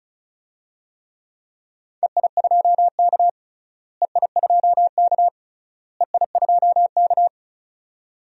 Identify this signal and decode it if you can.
Morse code, heard as a keyed tone, about 36 wpm, 700 Hz. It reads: EI2K EI2K EI2K
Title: DXCC-100 Call Signs - 4 Characters - Repeated 3x - 35wpm